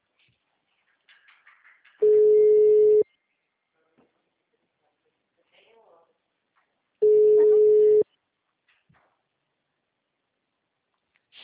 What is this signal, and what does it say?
Ringback tone (Europe, 425 Hz), with speech.
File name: out- -2028- - - .35932